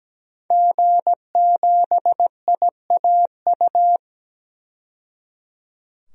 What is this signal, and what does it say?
Morse code, heard as a keyed tone, about 17 wpm, 700 Hz. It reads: G7IAU
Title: DXCC-100 Call Signs - 5 Characters - 17wpm